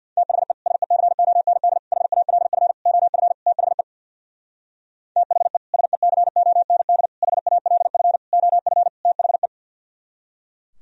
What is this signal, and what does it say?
Morse code, heard as a keyed tone, about 62 wpm, 700 Hz. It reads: THE SECOND HALF OF THE THE SECOND HALF OF THE
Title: Speed-Racing - Sets of 5 Words 50wpm